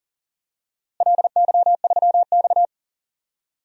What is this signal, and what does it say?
Morse code, heard as a keyed tone, about 40 wpm, 700 Hz. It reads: LY3X
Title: DXCC-100 Call Signs - 4 Characters - Repeated 3x - 40wpm